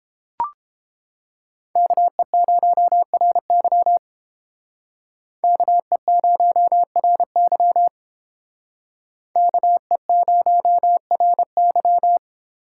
Morse code, heard as a keyed tone, about 32 wpm, 700 Hz. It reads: KE0RY KE0RY KE0RY